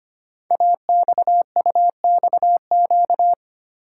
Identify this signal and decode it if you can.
Morse code, heard as a keyed tone, about 25 wpm, 700 Hz. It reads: AXUXQ
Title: Code Groups - 5 Letters - 25wpm